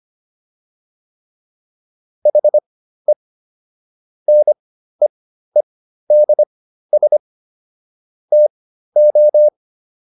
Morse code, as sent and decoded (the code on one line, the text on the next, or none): .... / -. . -.. ... / - ---
HE NEEDS TO